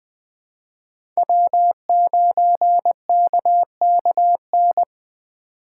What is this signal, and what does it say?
Morse code, heard as a keyed tone, about 20 wpm, 700 Hz. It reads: W9KKN
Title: USA Call Signs - 5 Characters - 2x Character Spacing - 20wpm